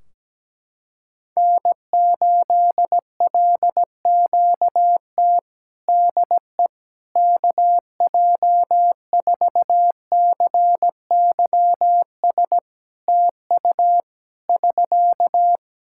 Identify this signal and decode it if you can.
Morse code, heard as a keyed tone, about 17 wpm, 700 Hz. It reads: N8LQT DE KJ4CYS TU <SK>
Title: K1USN SST QSO Practice 17wpm